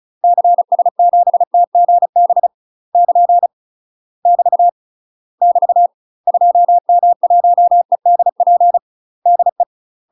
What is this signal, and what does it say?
Morse code, heard as a keyed tone, about 34 wpm, 700 Hz. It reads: CS7TGB <KN> <BT> <BT> 2M1EDP DE
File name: UK MASTER QSOs TRAINING / 35WPM / UK MASTER QSOs 0043